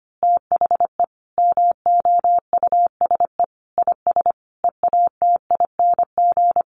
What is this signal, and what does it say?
Morse code, heard as a keyed tone, about 25 wpm, 700 Hz. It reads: THE MOUSE IS EATING
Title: Easy Phrases 25wpm